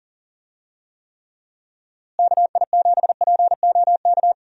Morse code, heard as a keyed tone, about 39 wpm, 700 Hz. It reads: KI7POK
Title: USA Call Signs - 6 Characters - Repeated 3x - 40wpm